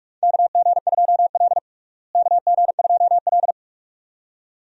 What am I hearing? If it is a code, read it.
Morse code, heard as a keyed tone, about 42 wpm, 700 Hz. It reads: KG2L KG2L
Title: USA Call Signs - 4 Characters - Repeated 2x - 45wpm